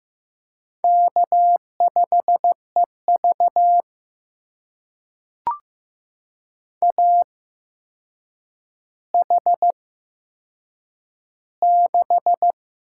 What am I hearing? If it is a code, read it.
Morse code, heard as a keyed tone, about 15 wpm, spaced as for 7 wpm, 700 Hz. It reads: K5EV AH6
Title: USA Call Signs - 4 Characters - 8x Character Spacing - 15wpm